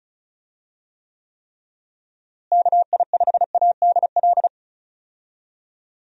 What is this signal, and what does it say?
Morse code, heard as a keyed tone, about 36 wpm, 700 Hz. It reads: KI5ADL
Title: USA Call Signs - 6 Characters - Repeated 3x - 35wpm